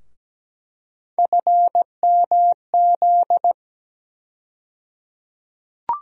Morse code, heard as a keyed tone, about 17 wpm, 700 Hz. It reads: FMZ
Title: USA Call Sign Postfixes - 3 Letters 17wpm